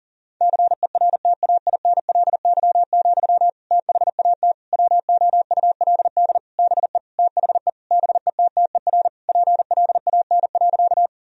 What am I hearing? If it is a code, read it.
Morse code, heard as a keyed tone, about 37 wpm, 700 Hz. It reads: CERTAINLY, THAT WOULD BE THE BETTER PLAN.
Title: Sentences from Top 1000 Words - Review Part 2 - 40wpm